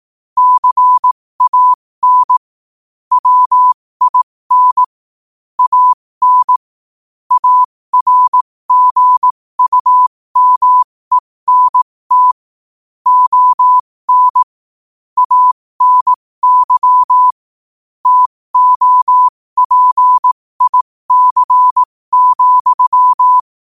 Morse code, heard as a keyed tone, about 18 wpm, spaced as for 15 wpm, 1000 Hz. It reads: CAN WIN AN ARGUMENT ON ANY TOPIC,
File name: 15wpm